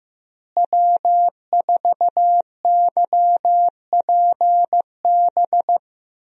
Morse code, heard as a keyed tone, about 15 wpm, 700 Hz. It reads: W4YPB
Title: USA Call Signs - 5 Characters - Repeated 2x - 15wpm